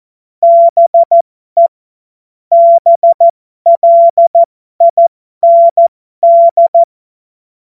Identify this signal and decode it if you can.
Morse code, heard as a keyed tone, about 14 wpm, spaced as for 12 wpm, 680 Hz. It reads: BE BLIND